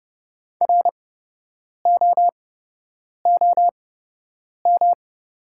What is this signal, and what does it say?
Morse code, heard as a keyed tone, about 30 wpm, spaced as for 8 wpm, 700 Hz. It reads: ROOM
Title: 8x Character Spacing - Top 300 Words - 30wpm